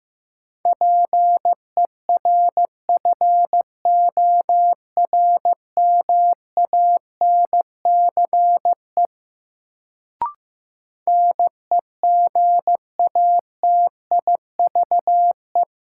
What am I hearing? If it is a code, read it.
Morse code, heard as a keyed tone, about 15 wpm, 700 Hz. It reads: PERFORMANCE NEGATIVE
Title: Sentences from Top 1000 Words - New Word Drill - 15wpm